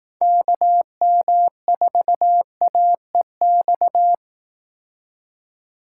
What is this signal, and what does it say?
Morse code, heard as a keyed tone, about 18 wpm, 700 Hz. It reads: KM4AEX
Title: Speed Racing - USA 2x3 Call Signs 15wpm